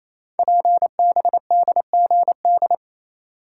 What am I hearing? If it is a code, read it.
Morse code, heard as a keyed tone, about 27 wpm, 700 Hz. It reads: PBDGD
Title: Code Groups - 5 Letters - 28wpm